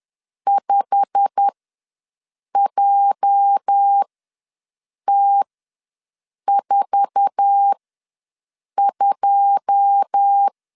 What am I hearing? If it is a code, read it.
Morse code, heard as a keyed tone, about 10 wpm, spaced as for 6 wpm, 800 Hz. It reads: 5JT42
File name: LEZIONI / 9 2 ws 10wpm 4130